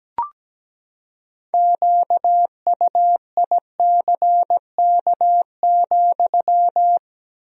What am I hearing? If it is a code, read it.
Morse code, heard as a keyed tone, about 17 wpm, 700 Hz. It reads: QUICK,
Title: 3x Word-Spacing - Sentences from Top 500 Words - 17wpm